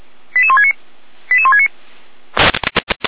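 Five-tone selective call (ZVEI), heard as quick runs of five tones, tones of about 70 ms. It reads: 88169 88169
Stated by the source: ZVEI five-tone selective call received by SDR, fire brigade alerting.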